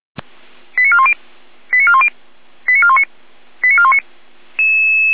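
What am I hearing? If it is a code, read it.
Five-tone selective call (ZVEI), heard as quick runs of five tones, tones of about 70 ms. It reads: 89410 89410 89419 89419
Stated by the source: ZVEI five-tone call 89097 received on an SDR, from feuerwehr-orlamuende.org fire brigade audio server.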